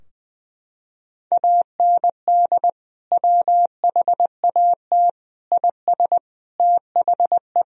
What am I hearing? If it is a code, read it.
Morse code, heard as a keyed tone, about 21 wpm, 700 Hz. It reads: AND WHAT IS THE
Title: Sentences from Top 1000 Words - Review Part 3 - 20wpm